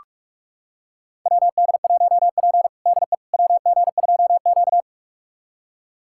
Morse code, heard as a keyed tone, about 42 wpm, 700 Hz. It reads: WD1P DE WG2Q
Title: USA Callsign DE Callsign - 4 Characters - 45wpm